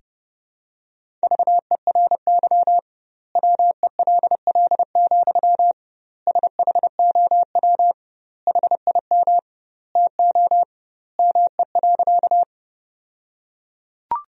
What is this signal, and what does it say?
Morse code, heard as a keyed tone, about 32 wpm, 700 Hz. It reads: VERY WELL, SHOW HIM TO ME.
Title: 2x Word-Spacing - Sentences from Top 200 Words - 30wpm